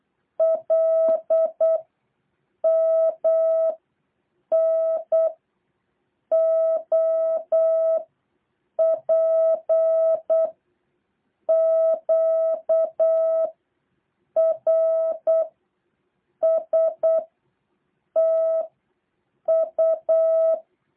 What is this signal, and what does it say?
Morse code, heard as a keyed tone, about 8 wpm, 650 Hz. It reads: L M N O P Q R S T U
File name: jh8cbh / 2023cw